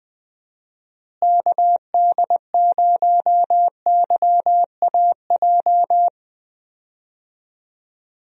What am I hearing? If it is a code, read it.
Morse code, heard as a keyed tone, about 20 wpm, 700 Hz. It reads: KD0YAJ
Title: USA Call Signs - 6 Characters - Repeated 3x - 20wpm